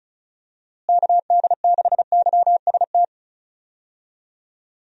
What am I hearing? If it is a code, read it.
Morse code, heard as a keyed tone, about 34 wpm, 700 Hz. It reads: KD6YST
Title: USA Call Signs - 6 Characters - Repeated 3x - 35wpm